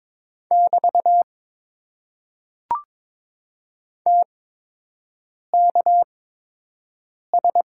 Morse code, heard as a keyed tone, about 23 wpm, spaced as for 6 wpm, 700 Hz. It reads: <BT> TKS
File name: Top 90 QSO Elements - 8x Character Spacing - 22wpm